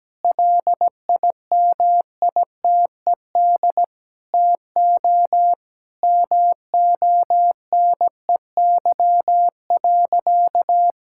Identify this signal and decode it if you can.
Morse code, heard as a keyed tone, about 17 wpm, 700 Hz. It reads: LIMITED TO MONEY.
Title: Sentences from Top 1000 Words - Review Part 6 - 17wpm